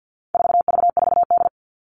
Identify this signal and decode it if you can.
Morse code, heard as a keyed tone, about 48 wpm, 700 Hz. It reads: 4V4D
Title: All Cut Numbers - 4-Digit 50wpm